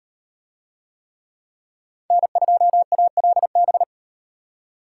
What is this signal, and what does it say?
Morse code, heard as a keyed tone, about 38 wpm, 700 Hz. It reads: N2ALB